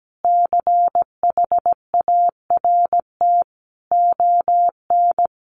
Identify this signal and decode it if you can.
Morse code, heard as a keyed tone, about 17 wpm, 700 Hz. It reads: CHART ON
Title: Sentences from Top 1300 Words 17wpm